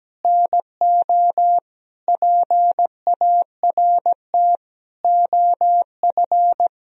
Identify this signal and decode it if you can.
Morse code, heard as a keyed tone, about 17 wpm, 700 Hz. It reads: NO PART OF
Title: Sentences from Top 1000 Words - Review Part 2 - 17wpm